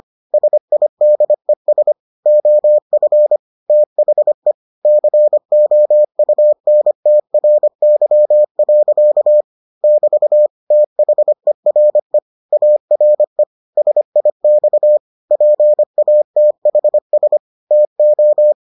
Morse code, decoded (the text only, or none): SIDES OF THE COUNTRY. <BT> THERE ARE SIX PATHS TO